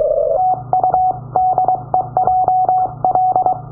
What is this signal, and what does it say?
Morse code, heard as a keyed tone, about 23 wpm, 740 Hz. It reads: TU DE JL
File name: iw0ffk / 2009 06 11 JL8GFB IW0FFK from the other side